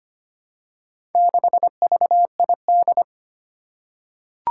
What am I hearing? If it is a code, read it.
Morse code, heard as a keyed tone, about 25 wpm, 700 Hz. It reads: 6VID
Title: Four Challenging Letter-Number Combinations 25wpm